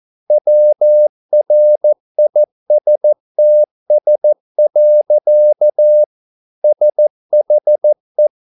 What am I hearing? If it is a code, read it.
Morse code, heard as a keyed tone, about 14 wpm, 600 Hz. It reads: WRISTS. SHE